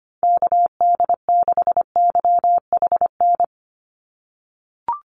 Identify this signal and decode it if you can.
Morse code, heard as a keyed tone, about 25 wpm, 700 Hz. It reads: KD6YHN
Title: USA Call Signs 25wpm